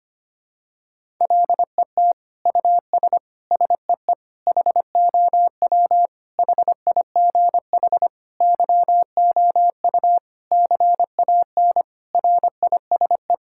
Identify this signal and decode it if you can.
Morse code, heard as a keyed tone, about 25 wpm, 700 Hz. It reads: LET US SEE HOW HIGH YOU CAN RISE
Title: Sentences from Top 1000 Words - Review Part 6 - 25wpm